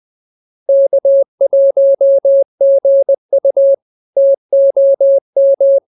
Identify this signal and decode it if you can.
Morse code, heard as a keyed tone, about 20 wpm, 550 Hz. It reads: K1GU TOM
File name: CWT-206-20